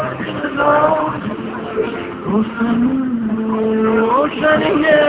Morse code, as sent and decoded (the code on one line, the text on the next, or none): none